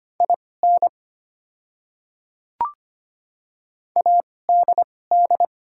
Morse code, heard as a keyed tone, about 25 wpm, spaced as for 18 wpm, 700 Hz. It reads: IN ADD
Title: 2x Character Spacing - Top 500 Words - 25wpm